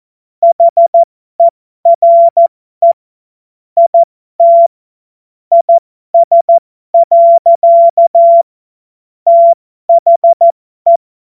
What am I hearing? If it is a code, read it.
Morse code, heard as a keyed tone, about 14 wpm, spaced as for 12 wpm, 680 Hz. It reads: HERE IT IS. THE
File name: Morsekurs / Witze